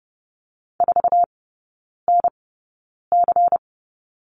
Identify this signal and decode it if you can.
Morse code, heard as a keyed tone, about 30 wpm, spaced as for 9 wpm, 690 Hz. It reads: 4NC
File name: Lesson 16e - Letter c - Three Characters 30wpm